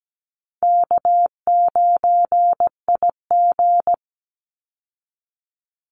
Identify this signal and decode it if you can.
Morse code, heard as a keyed tone, about 17 wpm, 700 Hz. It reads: K9IG
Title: Easy Call Signs Repeated 3x - 17wpm